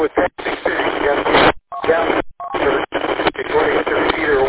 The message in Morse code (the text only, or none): none